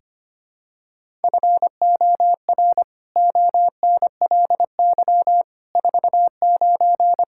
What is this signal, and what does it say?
Morse code, heard as a keyed tone, about 25 wpm, 700 Hz. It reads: FOR ONLY 49
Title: Pangram Challenge v2 - 25wpm